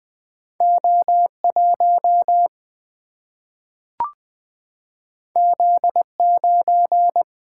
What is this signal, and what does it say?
Morse code, heard as a keyed tone, about 20 wpm, 700 Hz. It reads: O 1 Z 9
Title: Single-Letter Single-Number 20wpm